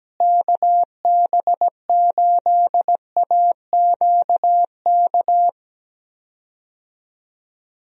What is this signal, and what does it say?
Morse code, heard as a keyed tone, about 17 wpm, 700 Hz. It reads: KB8AQK
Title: USA Call Signs - 6 Characters - Repeated 3x - 17wpm